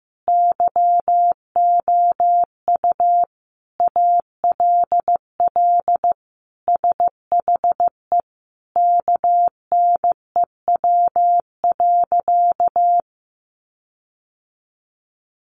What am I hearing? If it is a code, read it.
Morse code, heard as a keyed tone, about 15 wpm, 700 Hz. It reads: YOU ALL SHE KNEW.
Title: Sentences from Top 1300 Words 15wpm